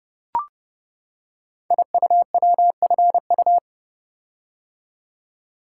Morse code, heard as a keyed tone, about 30 wpm, 700 Hz. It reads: IUWFU
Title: Code Groups - 5 Letters - 30wpm